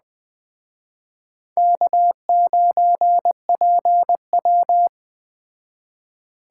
Morse code, heard as a keyed tone, about 21 wpm, 700 Hz. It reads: K9PW
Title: Speed Racing - USA 1x2 Call Signs 17wpm